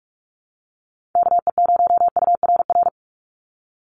Morse code, heard as a keyed tone, about 44 wpm, 700 Hz. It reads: KE0URR
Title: USA Call Signs 45wpm